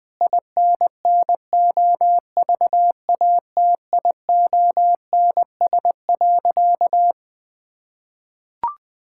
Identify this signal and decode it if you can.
Morse code, heard as a keyed tone, about 20 wpm, 700 Hz. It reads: INNOVATIONS.
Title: Sentences with 7-Word Phrases - 20wpm